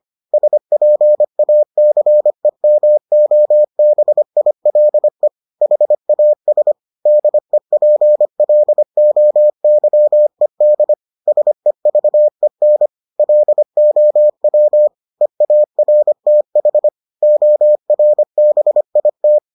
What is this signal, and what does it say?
Morse code, heard as a keyed tone, about 25 wpm, 600 Hz. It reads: SPACEMOBILE HAS DEPLOYED SEVEN LOW EARTH ORBIT